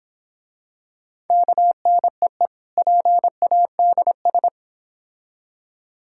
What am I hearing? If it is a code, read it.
Morse code, heard as a keyed tone, about 26 wpm, 700 Hz. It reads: KNEE PADS